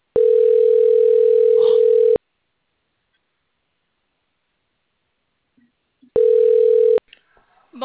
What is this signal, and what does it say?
Ringback tone (North America), 440 + 480 Hz.